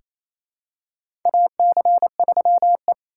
Morse code, heard as a keyed tone, about 27 wpm, 700 Hz. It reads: AC3E
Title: USA Call Signs - 4 Characters - 3x Character Spacing - 28wpm